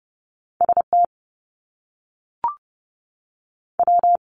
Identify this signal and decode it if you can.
Morse code, heard as a keyed tone, about 30 wpm, 700 Hz. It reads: S T W
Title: Top Two Letter Combinations in Words 30wpm